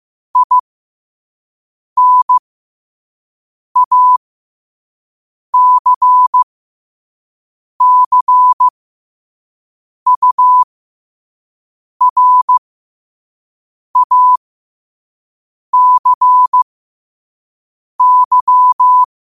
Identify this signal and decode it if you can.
Morse code, heard as a keyed tone, about 15 wpm, spaced as for 5 wpm, 1000 Hz. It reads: INACCURACY
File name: qotdfiles / 5wpm